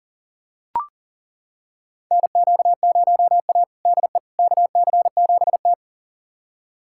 Morse code, heard as a keyed tone, about 40 wpm, 700 Hz. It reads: NQ0A DE KC7T
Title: USA Callsign DE Callsign - 4 Characters - 40wpm